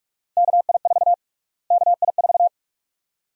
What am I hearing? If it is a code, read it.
Morse code, heard as a keyed tone, about 46 wpm, 700 Hz. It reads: KI4 KI4
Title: USA Call Sign Prefixes - Repeated 2x - 45wpm